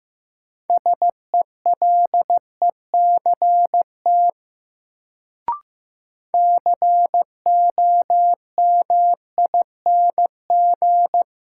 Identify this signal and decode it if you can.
Morse code, heard as a keyed tone, about 15 wpm, 700 Hz. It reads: SELECT COMING
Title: Sentences from Top 1300 Words - New Word Drill - 15wpm